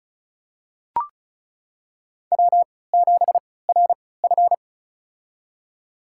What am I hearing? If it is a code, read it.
Morse code, heard as a keyed tone, about 36 wpm, spaced as for 20 wpm, 700 Hz. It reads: W7RF
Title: USA Call Signs - 3x Character Spacing - 35wpm